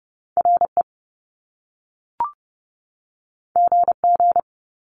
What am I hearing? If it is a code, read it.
Morse code, heard as a keyed tone, about 30 wpm, 700 Hz. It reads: RE GG
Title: Top Two Letter Combinations in Words 30wpm